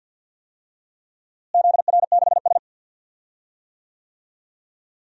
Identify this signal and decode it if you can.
Morse code, heard as a keyed tone, about 50 wpm, 700 Hz. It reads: ZR6S
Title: DXCC-100 Call Signs - 4 Characters - Repeated 2x - 50wpm